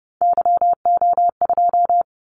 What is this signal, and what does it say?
Morse code, heard as a keyed tone, about 30 wpm, 700 Hz. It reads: YO2
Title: Callsign Prefixes for Top 25 DXCC - 30wpm